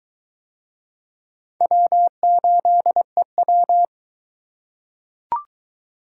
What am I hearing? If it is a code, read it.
Morse code, heard as a keyed tone, about 21 wpm, 700 Hz. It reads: W8EW